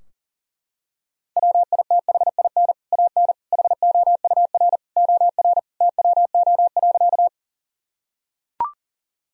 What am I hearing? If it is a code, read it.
Morse code, heard as a keyed tone, about 40 wpm, 700 Hz. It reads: WITHIN AN HOUR OR TWO.
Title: Sentences from Top 1000 Words - Review Part 4 - 40wpm